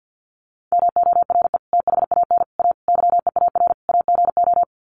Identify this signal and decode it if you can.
Morse code, heard as a keyed tone, about 50 wpm, 700 Hz. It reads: MORE THAN A YEAR AGO
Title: Sets of 5 Words Encore2 50wpm